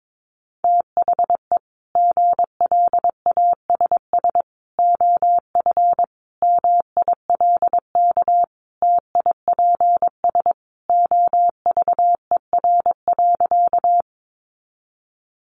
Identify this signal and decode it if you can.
Morse code, heard as a keyed tone, about 22 wpm, 700 Hz. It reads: THE GLASS OF MILK TIPS OVER.
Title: Additional Easy Phrases 22wpm